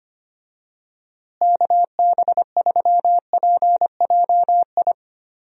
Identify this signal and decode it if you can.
Morse code, heard as a keyed tone, about 25 wpm, 700 Hz. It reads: KB3PJI
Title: USA Call Signs - 6 Characters - Repeated 2x - 25wpm